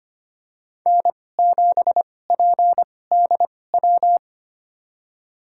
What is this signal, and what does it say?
Morse code, heard as a keyed tone, about 25 wpm, spaced as for 18 wpm, 700 Hz. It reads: N7PDW
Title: USA Call Signs - 5 Characters - 2x Character Spacing - 25wpm